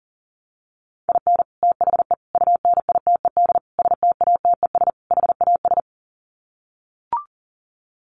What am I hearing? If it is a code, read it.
Morse code, heard as a keyed tone, about 40 wpm, 700 Hz. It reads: IN THE UNITED STATES HAS
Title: Sets of 5 Words Encore2 40wpm